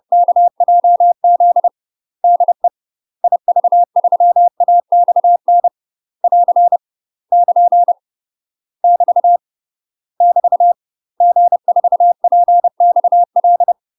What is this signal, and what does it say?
Morse code, heard as a keyed tone, about 30 wpm, 700 Hz. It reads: KJZ DE IV3AXN <AR> <KN> <BT> <BT> G4PXL
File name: UK MASTER QSOs TRAINING / 30WPM / UK MASTER QSOs 0024